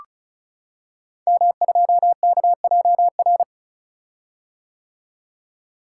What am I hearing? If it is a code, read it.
Morse code, heard as a keyed tone, about 36 wpm, 700 Hz. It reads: M2KJR
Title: Code Groups - 5 Letter:Numbers - 35wpm